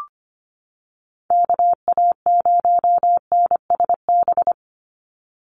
Morse code, heard as a keyed tone, about 25 wpm, 700 Hz. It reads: KA0NSB
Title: USA Call Signs 25wpm